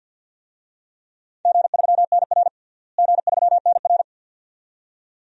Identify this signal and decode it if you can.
Morse code, heard as a keyed tone, about 52 wpm, 700 Hz. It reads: G3NR G3NR
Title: DXCC-100 Call Signs - 4 Characters - Repeated 2x - 50wpm